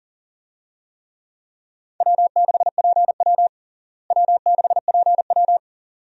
Morse code, heard as a keyed tone, about 40 wpm, 700 Hz. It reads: W6PW W6PW
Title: USA Call Signs - 4 Characters - Repeated 2x - 40wpm